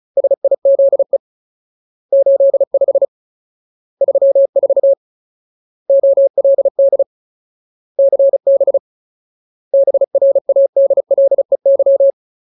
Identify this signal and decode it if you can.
Morse code, heard as a keyed tone, about 34 wpm, 550 Hz. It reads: SIZE 85 34 OLD CB BRADLEY
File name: Ultimate Word List / 35WPM Ultimate